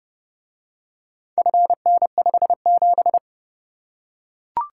Morse code, heard as a keyed tone, about 30 wpm, 700 Hz. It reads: FN57